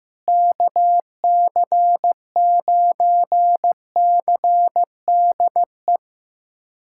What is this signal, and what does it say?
Morse code, heard as a keyed tone, about 15 wpm, 700 Hz. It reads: KC9CDE
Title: USA Callsign DE Callsign 15wpm